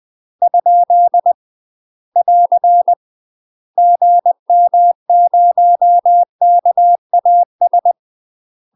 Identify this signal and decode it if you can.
Morse code, heard as a keyed tone, about 20 wpm, 700 Hz. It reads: ? <AR> GM0KAS
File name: UK MASTER QSOs TRAINING / 20WPM / UK MASTER QSOs 0072